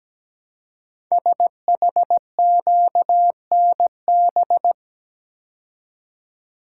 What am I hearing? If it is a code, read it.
Morse code, heard as a keyed tone, about 17 wpm, 700 Hz. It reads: SHQNB
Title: Code Groups - 5 Letters - 17wpm